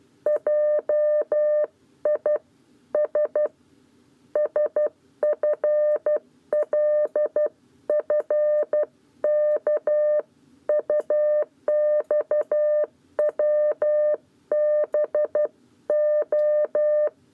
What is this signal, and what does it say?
Morse code, heard as a keyed tone, about 12 wpm, 600 Hz. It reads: JI S SFLFK UXWBO
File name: 7b lezione cw 1